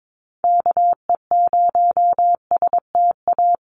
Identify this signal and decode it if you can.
Morse code, heard as a keyed tone, about 22 wpm, 700 Hz. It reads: KE0STA
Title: USA Call Signs 22wpm